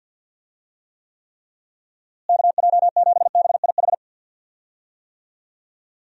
Morse code, heard as a keyed tone, about 51 wpm, 700 Hz. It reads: KJ7BIH
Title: USA Call Signs - 6 Characters - Repeated 3x - 50wpm